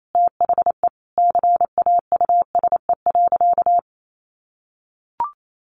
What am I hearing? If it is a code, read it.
Morse code, heard as a keyed tone, about 28 wpm, 700 Hz. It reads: THE CAUSE.
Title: Sentences from Top 700 Words 28wpm